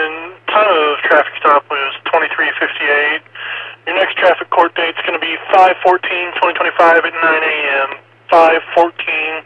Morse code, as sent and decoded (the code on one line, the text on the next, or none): none